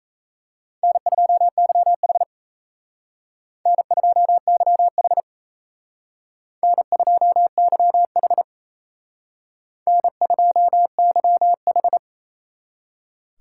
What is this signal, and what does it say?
Morse code, heard as a keyed tone, about 37 wpm, 700 Hz. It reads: N2YH N2YH N2YH N2YH